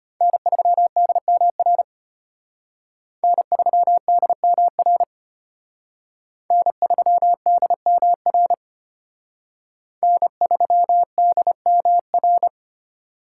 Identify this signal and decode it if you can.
Morse code, heard as a keyed tone, about 31 wpm, 700 Hz. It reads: N3DMR N3DMR N3DMR N3DMR
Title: Speed Racing - USA 1x3 Call Signs 25wpm